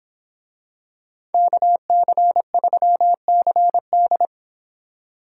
Morse code, heard as a keyed tone, about 27 wpm, 700 Hz. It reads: KC3CD
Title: Speed Racing - USA Call Signs 22wpm